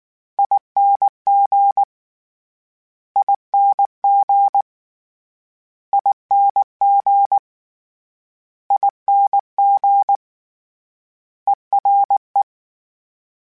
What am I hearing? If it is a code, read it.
Morse code, heard as a keyed tone, about 19 wpm, 800 Hz. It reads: ING ING ING ING ERE
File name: Special INGs / Special INGs-mp3s / Ere ing @18